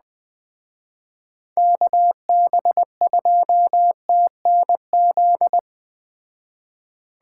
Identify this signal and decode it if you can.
Morse code, heard as a keyed tone, about 20 wpm, 700 Hz. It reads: KB2TNZ
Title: Speed Racing - USA 2x3 Call Signs 17wpm